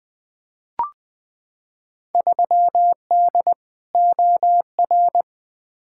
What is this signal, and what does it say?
Morse code, heard as a keyed tone, about 20 wpm, 700 Hz. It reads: 3D OR
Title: Field Day Exchanges - 20wpm